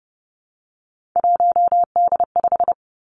Morse code, heard as a keyed tone, about 30 wpm, 700 Hz. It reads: 1D5
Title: Three Challenging Letter-Number Combinations 30wpm